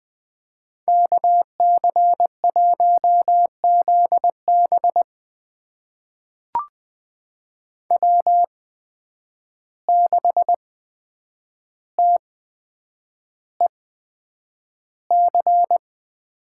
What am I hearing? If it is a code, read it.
Morse code, heard as a keyed tone, about 20 wpm, spaced as for 9 wpm, 700 Hz. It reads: KC1ZB W6TEC
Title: USA Call Signs - 5 Characters - 8x Character Spacing - 20wpm